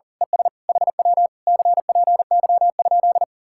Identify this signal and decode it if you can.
Morse code, heard as a keyed tone, about 38 wpm, 700 Hz. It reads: ES HW CPY?
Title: Speed-Racing Common QSO Phrases - 40wpm